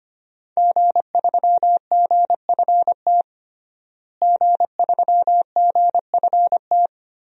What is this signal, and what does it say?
Morse code, heard as a keyed tone, about 25 wpm, 700 Hz. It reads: G3GFT G3GFT
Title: DXCC-100 Call Signs - 5 Characters - Repeated 2x - 25wpm